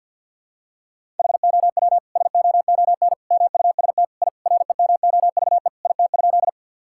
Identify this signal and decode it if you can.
Morse code, heard as a keyed tone, about 52 wpm, 700 Hz. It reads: HOW SOON MUST I REMOVE IT?
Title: Sentences from Top 1000 Words - Review Part 5 - 50wpm